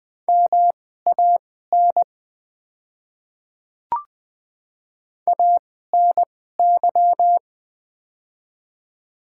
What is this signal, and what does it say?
Morse code, heard as a keyed tone, about 20 wpm, spaced as for 14 wpm, 700 Hz. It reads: MAN ANY
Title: 2x Character Spacing - Top 500 Words - 20wpm